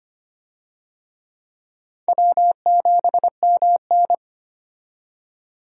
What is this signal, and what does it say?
Morse code, heard as a keyed tone, about 25 wpm, 700 Hz. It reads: W7MN
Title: USA Call Signs - 4 Characters - 25wpm